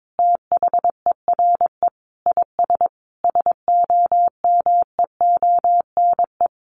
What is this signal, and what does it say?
Morse code, heard as a keyed tone, about 22 wpm, 700 Hz. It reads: THERE IS SOMEONE
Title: Sentences from Top 1500 Words 22wpm